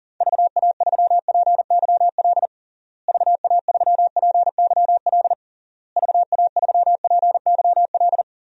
Morse code, heard as a keyed tone, about 40 wpm, 700 Hz. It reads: VA3PYL VA3PYL VA3PYL
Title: DXCC-100 Call Signs - 6 Characters - Repeated 3x - 40wpm